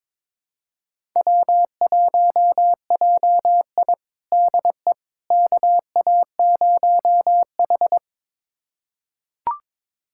Morse code, heard as a keyed tone, about 21 wpm, 700 Hz. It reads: W1JI DE KA0H
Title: USA Callsign DE Callsign - 4 Characters - 22wpm